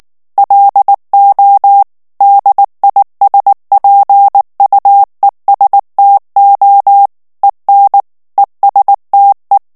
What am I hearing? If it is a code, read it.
Morse code, heard as a keyed tone, about 19 wpm, 800 Hz. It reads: LO DISPUESTO EN ESTE
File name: morse mp3 / ESPAGNOL TEXTE EN CLAIR 2 VITESSE 18 MOTS MINUTES